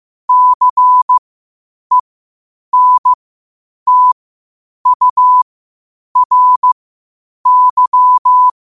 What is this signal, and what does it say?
Morse code, heard as a keyed tone, about 15 wpm, spaced as for 8 wpm, 1000 Hz. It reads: CENTURY
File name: qotdfiles / 8wpm